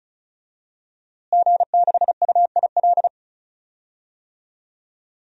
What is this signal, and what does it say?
Morse code, heard as a keyed tone, about 36 wpm, 700 Hz. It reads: G6UIL
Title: DXCC-100 Call Signs - 5 Characters - Repeated 3x - 35wpm